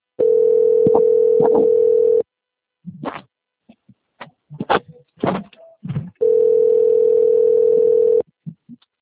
Ringback tone (North America), 440 + 480 Hz.
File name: out- - - - - .16715